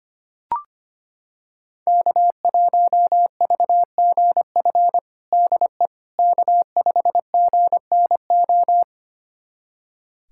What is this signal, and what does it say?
Morse code, heard as a keyed tone, about 25 wpm, 700 Hz. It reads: K1VGF DE K5GNO
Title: USA Callsign DE Callsign - 5 Characters - 25wpm